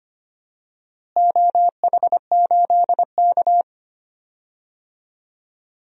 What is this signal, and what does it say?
Morse code, heard as a keyed tone, about 25 wpm, 700 Hz. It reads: OH8K
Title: DXCC-100 Call Signs - 4 Characters - Repeated 3x - 25wpm